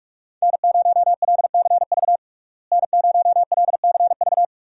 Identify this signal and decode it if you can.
Morse code, heard as a keyed tone, about 42 wpm, 700 Hz. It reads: N0LCV N0LCV
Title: USA Call Signs - 5 Characters - Repeated 3x - 45wpm